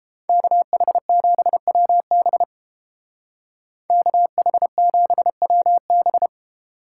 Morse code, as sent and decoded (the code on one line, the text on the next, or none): -.- .... --... .-- -... / -.- .... --... .-- -...
KH7WB KH7WB